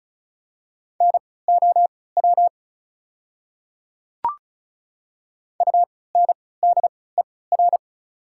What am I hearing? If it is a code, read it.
Morse code, heard as a keyed tone, about 36 wpm, spaced as for 20 wpm, 700 Hz. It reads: NOW UNDER
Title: 3x Character Spacing - Top 200 Words - 35wpm